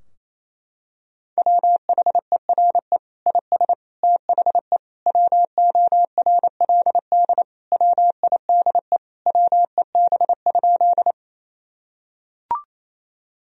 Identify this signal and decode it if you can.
Morse code, heard as a keyed tone, about 27 wpm, 700 Hz. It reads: WHERE IS THE WORLD WIDE WEB?
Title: Sentences from Top 1000 Words - Review Part 4 - 28wpm